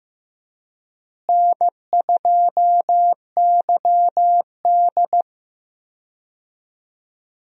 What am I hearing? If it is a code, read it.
Morse code, heard as a keyed tone, about 15 wpm, 700 Hz. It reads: N2YD
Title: USA Call Signs - 4 Characters - Repeated 3x - 15wpm